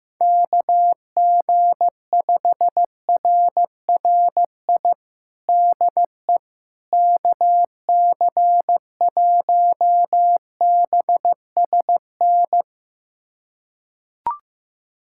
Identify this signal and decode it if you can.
Morse code, heard as a keyed tone, about 15 wpm, 700 Hz. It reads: KG5RRI DE KC1BSN
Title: USA Callsign DE Callsign 15wpm